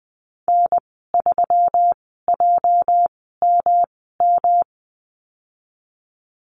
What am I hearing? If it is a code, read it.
Morse code, heard as a keyed tone, about 20 wpm, spaced as for 15 wpm, 700 Hz. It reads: N3JMM
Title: 207 Lesson 33g - Letter j - Call Signs 20wpm